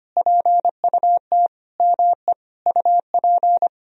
Morse code, heard as a keyed tone, about 25 wpm, 700 Hz. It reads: PUT ME UP
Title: Sentences from Top 1000 Words - Review Part 3 - 25wpm